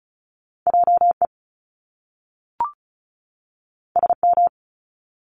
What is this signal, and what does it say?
Morse code, heard as a keyed tone, about 36 wpm, 700 Hz. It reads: JE SM
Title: Top Two Letter Combinations in Words 35wpm